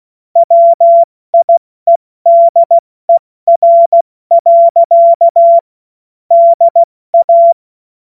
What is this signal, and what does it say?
Morse code, heard as a keyed tone, about 16 wpm, 680 Hz. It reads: WIEDER. DA
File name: Morsekurs / Witze / Witz 16. August 70 F80